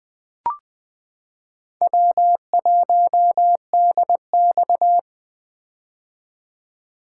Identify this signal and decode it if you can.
Morse code, heard as a keyed tone, about 20 wpm, 700 Hz. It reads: W1DX
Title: K1USN SST QSO Practice - 20wpm